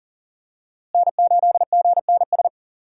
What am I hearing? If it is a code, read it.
Morse code, heard as a keyed tone, about 39 wpm, 700 Hz. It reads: N8GNS
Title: USA Call Signs - 5 Characters - Repeated 2x - 40wpm